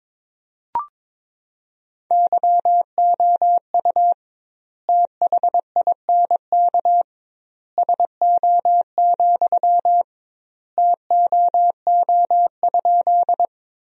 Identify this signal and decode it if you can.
Morse code, heard as a keyed tone, about 22 wpm, 700 Hz. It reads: YOU THINK SO, TOO?
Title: 2x Word-Spacing - Sentences from Top 200 Words - 22wpm